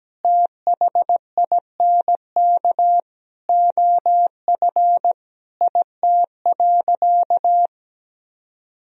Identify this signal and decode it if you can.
Morse code, heard as a keyed tone, about 17 wpm, 700 Hz. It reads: THINK OF IT.
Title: Sentences from Top 200 Words and Abbreviations 17wpm